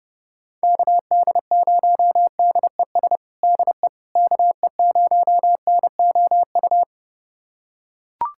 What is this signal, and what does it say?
Morse code, heard as a keyed tone, about 30 wpm, 700 Hz. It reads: KD0DES DE KE0NOU
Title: USA Callsign DE Callsign - 6 Characters - 30wpm